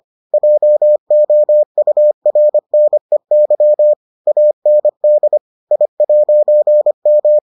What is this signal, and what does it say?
Morse code, heard as a keyed tone, about 25 wpm, 600 Hz. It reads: JOURNEY AND I'M